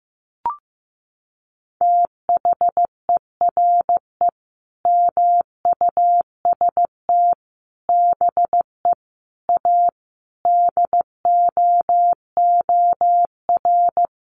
Morse code, heard as a keyed tone, about 15 wpm, 700 Hz. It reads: THERE MUST BE A DOOR
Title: Sentences from Top 1400 Words 15wpm